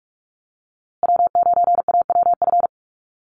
Morse code, heard as a keyed tone, about 44 wpm, 700 Hz. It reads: W9AWF
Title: USA Call Signs Repeated 2x 45wpm